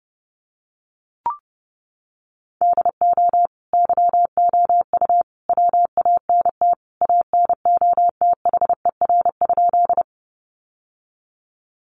Morse code, heard as a keyed tone, about 30 wpm, 700 Hz. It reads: DO YOU WANT ANOTHER?
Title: Sentences from Top 600 Words 30wpm